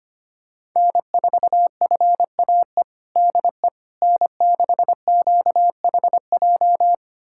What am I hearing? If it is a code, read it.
Morse code, heard as a keyed tone, about 25 wpm, 700 Hz. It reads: N4FAE DE N6QHJ
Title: USA Callsign DE Callsign - 5 Characters - 25wpm